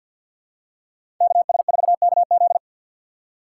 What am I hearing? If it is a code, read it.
Morse code, heard as a keyed tone, about 52 wpm, 700 Hz. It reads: KS4KZ
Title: USA Call Signs - 5 Characters - 6x Character Spacing - 50wpm